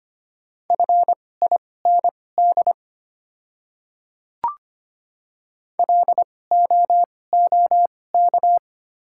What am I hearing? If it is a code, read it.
Morse code, heard as a keyed tone, about 25 wpm, spaced as for 18 wpm, 690 Hz. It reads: FIND LOOK
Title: 2x Character Spacing - Top 200 Words - 25wpm